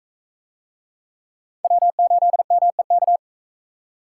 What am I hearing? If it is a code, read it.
Morse code, heard as a keyed tone, about 40 wpm, 700 Hz. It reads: W8MEK